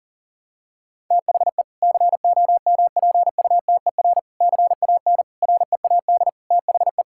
Morse code, heard as a keyed tone, about 40 wpm, 700 Hz. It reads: THE COMPUTER CAN READ THE
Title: Sentences from Top 1000 Words - Review Part 6 - 40wpm